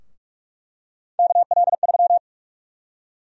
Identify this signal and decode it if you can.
Morse code, heard as a keyed tone, about 46 wpm, 700 Hz. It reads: KL3
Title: USA Call Sign Prefixes - 45wpm